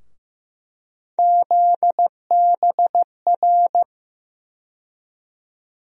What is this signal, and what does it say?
Morse code, heard as a keyed tone, about 15 wpm, 700 Hz. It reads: ZBR
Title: USA Call Sign Postfixes - 3 Letters 15wpm